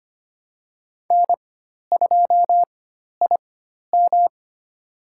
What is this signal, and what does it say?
Morse code, heard as a keyed tone, about 25 wpm, spaced as for 12 wpm, 700 Hz. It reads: N2IM
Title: USA Call Signs - 4x Character Spacing - 25wpm